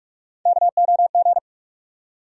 Morse code, heard as a keyed tone, about 47 wpm, 700 Hz. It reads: KOG
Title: USA Call Sign Postfixes - 3 Letters 45wpm